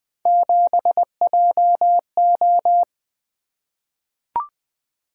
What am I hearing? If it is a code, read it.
Morse code, heard as a keyed tone, about 20 wpm, 700 Hz. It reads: 7JO
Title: Three Letter-Number Combinations - 20wpm